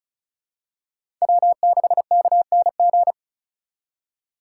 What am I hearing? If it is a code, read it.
Morse code, heard as a keyed tone, about 36 wpm, 700 Hz. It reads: W6KNG
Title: USA Call Signs - 5 Characters - 2x Character Spacing - 35wpm